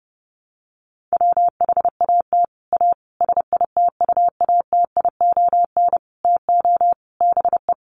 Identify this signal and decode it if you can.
Morse code, heard as a keyed tone, about 30 wpm, 700 Hz. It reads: WHAT A SITUATION TO BE
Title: Sentences from Top 1200 Words 30wpm